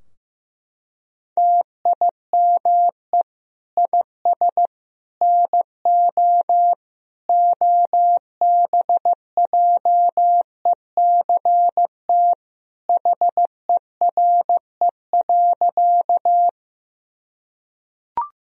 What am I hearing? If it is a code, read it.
Morse code, heard as a keyed tone, about 15 wpm, 700 Hz. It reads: TIME IS NO OBJECT HERE.
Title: Sentences from Top 1000 Words - Review Part 3 - 15wpm